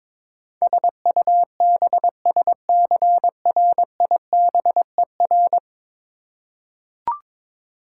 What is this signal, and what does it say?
Morse code, heard as a keyed tone, about 21 wpm, 700 Hz. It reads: SUBSCRIBER
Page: morse-code-ninja-practice-22wpm